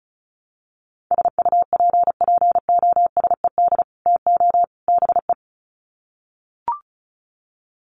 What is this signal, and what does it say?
Morse code, heard as a keyed tone, about 34 wpm, 700 Hz. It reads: SUPPOSED TO BE
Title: Sets of 3 Words 35wpm